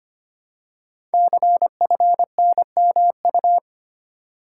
Morse code, heard as a keyed tone, about 25 wpm, 700 Hz. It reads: CFNMU
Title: Code Groups - 5 Letters - 25wpm